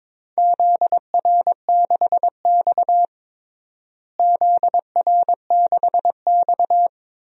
Morse code, heard as a keyed tone, about 21 wpm, 700 Hz. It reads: ZR6X ZR6X
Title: DXCC-100 Call Signs - 4 Characters - Repeated 3x - 22wpm